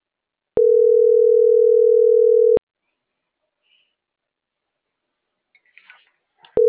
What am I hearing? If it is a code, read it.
Ringback tone (North America), 440 + 480 Hz.